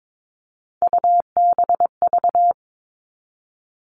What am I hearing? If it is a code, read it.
Morse code, heard as a keyed tone, about 22 wpm, 700 Hz. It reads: UBV
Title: All Cut Numbers - 3-Digit 22wpm